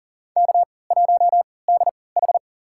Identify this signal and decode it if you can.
Morse code, heard as a keyed tone, about 40 wpm, spaced as for 23 wpm, 700 Hz. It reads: K1DH
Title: USA Call Signs - 4 Characters - 3x Character Spacing - 40wpm